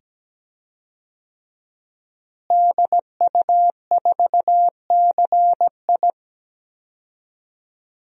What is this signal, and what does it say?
Morse code, heard as a keyed tone, about 17 wpm, 700 Hz. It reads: DU4CI
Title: DXCC-100 Call Signs - 5 Characters - Repeated 2x - 17wpm